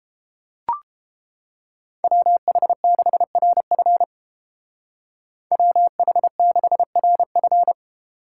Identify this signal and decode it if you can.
Morse code, heard as a keyed tone, about 30 wpm, 700 Hz. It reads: WH6RF WH6RF